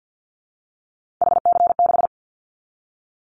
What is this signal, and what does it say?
Morse code, heard as a keyed tone, about 50 wpm, 700 Hz. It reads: HC6
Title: Callsign Prefixes for Top 100 DXCC - 50wpm